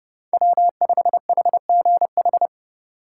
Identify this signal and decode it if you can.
Morse code, heard as a keyed tone, about 32 wpm, 700 Hz. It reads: W5HGH